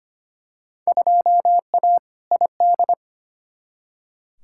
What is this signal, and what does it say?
Morse code, heard as a keyed tone, about 25 wpm, 700 Hz. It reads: 2A ID
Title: Field Day Exchanges - 25wpm